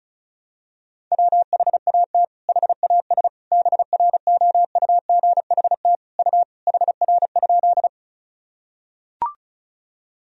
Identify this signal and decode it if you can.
Morse code, heard as a keyed tone, about 34 wpm, 700 Hz. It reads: WHAT HAS BROUGHT U HR?
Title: Sentences from Top 500 Words and Abbreviations 35wpm